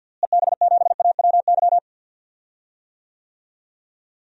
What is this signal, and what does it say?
Morse code, heard as a keyed tone, about 47 wpm, 700 Hz. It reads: EB7AWY
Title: DXCC-100 Call Signs - 6 Characters - Repeated 2x - 50wpm